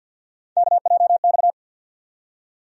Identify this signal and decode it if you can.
Morse code, heard as a keyed tone, about 47 wpm, 700 Hz. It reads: KJX
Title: USA Call Sign Postfixes - 3 Letters 50wpm